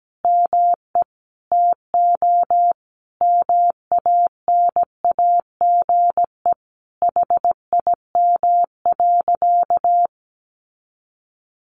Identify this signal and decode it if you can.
Morse code, heard as a keyed tone, about 17 wpm, 700 Hz. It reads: ME TO MANAGE HIM.
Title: Sentences from Top 1400 Words 17wpm